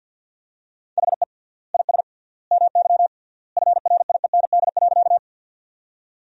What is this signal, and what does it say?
Morse code, heard as a keyed tone, about 48 wpm, 700 Hz. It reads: HE IS MY FRIEND.
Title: 3x Word-Spacing - Sentences from Top 300 Words - 50wpm